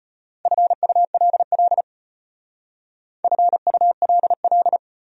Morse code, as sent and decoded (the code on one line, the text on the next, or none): ..-. ..- .-.. .-.. / ..-. ..- .-.. .-..
FULL FULL